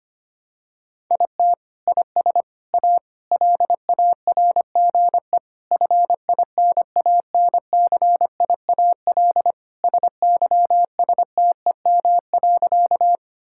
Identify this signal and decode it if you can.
Morse code, heard as a keyed tone, about 25 wpm, 700 Hz. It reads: IT IS A LARGE FINANCIAL SYSTEM.
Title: Sentences from Top 1000 Words - Review Part 5 - 25wpm